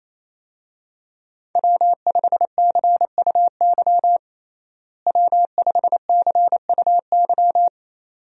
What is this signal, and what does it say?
Morse code, heard as a keyed tone, about 28 wpm, 700 Hz. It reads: W5CUY W5CUY
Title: USA Call Signs - 5 Characters - Repeated 2x - 28wpm